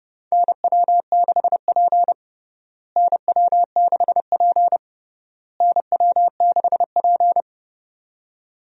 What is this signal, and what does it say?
Morse code, heard as a keyed tone, about 32 wpm, 700 Hz. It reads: NW6P NW6P NW6P
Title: USA Call Signs - 4 Characters - Repeated 3x - 30wpm